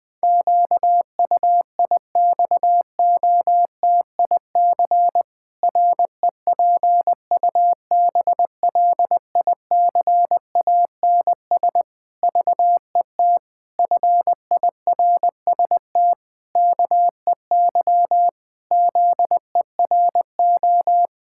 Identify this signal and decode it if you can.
Morse code, heard as a keyed tone, about 20 wpm, 700 Hz. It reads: QUIXOTIC REPUBLICANS VET FIRST KEY ZERO
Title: Pangram Challenge v2 - 20wpm